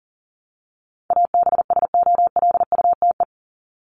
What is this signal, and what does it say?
Morse code, heard as a keyed tone, about 40 wpm, 700 Hz. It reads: ABSOLUTE
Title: LY Suffix - Top 100 Words with Base Words 40wpm